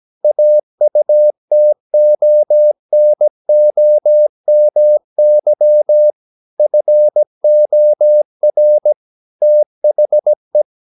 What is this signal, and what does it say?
Morse code, heard as a keyed tone, about 17 wpm, 600 Hz. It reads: AUTONOMY FOR THE
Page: Romanen Never af Ken Follet på engelsk.